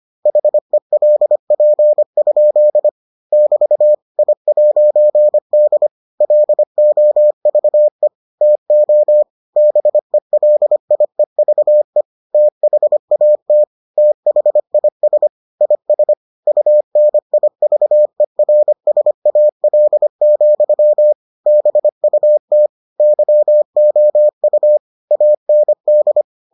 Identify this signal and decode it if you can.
Morse code, heard as a keyed tone, about 25 wpm, 600 Hz. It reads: HELP? <BT> I'D LOVE TO BELIEVE THAT THIS IS UNIVERSAL, BUT YOU AND